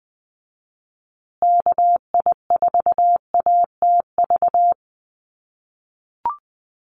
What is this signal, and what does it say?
Morse code, heard as a keyed tone, about 20 wpm, 700 Hz. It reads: KI4ATV
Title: USA Call Signs Repeated 3x 20wpm